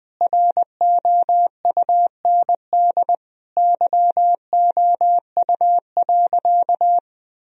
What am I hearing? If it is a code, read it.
Morse code, heard as a keyed tone, about 20 wpm, 700 Hz. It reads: ROUND YOU.
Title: Sentences from Top 1000 Words - Review Part 1 - 20wpm